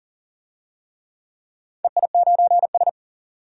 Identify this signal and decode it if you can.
Morse code, heard as a keyed tone, about 40 wpm, 700 Hz. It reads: EI9S
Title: DXCC-100 Call Signs - 4 Characters - 40wpm